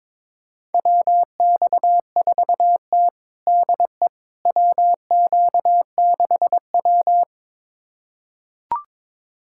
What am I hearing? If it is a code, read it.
Morse code, heard as a keyed tone, about 21 wpm, 700 Hz. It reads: WX4T DE WQ6W
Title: USA Callsign DE Callsign - 4 Characters - 22wpm